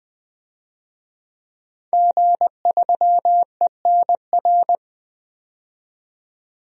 Morse code, heard as a keyed tone, about 20 wpm, 700 Hz. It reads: G3ENR
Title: DXCC-100 Call Signs - 5 Characters - 20wpm